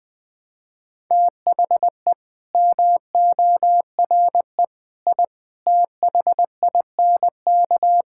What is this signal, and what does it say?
Morse code, heard as a keyed tone, about 20 wpm, 700 Hz. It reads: THE MORE I THINK
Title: Sets of 9 Words - 20wpm